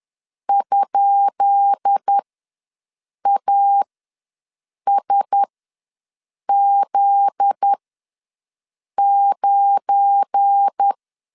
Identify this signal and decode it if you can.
Morse code, heard as a keyed tone, about 10 wpm, spaced as for 6 wpm, 800 Hz. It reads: ?ASZ9
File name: LEZIONI / 10 SIGNS / 10D COMA ws 10wpm 4314